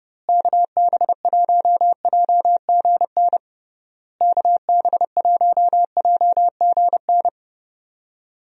Morse code, heard as a keyed tone, about 32 wpm, 700 Hz. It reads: KB1JGN KB1JGN
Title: USA Call Signs - 6 Characters - Repeated 3x - 30wpm